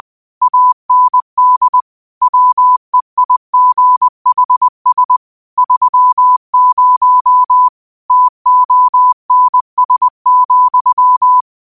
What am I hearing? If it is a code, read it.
Morse code, heard as a keyed tone, about 20 wpm, 1000 Hz. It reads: AND WEIGHS 30 TONS,